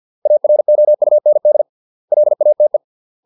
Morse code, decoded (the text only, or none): AROUND LATE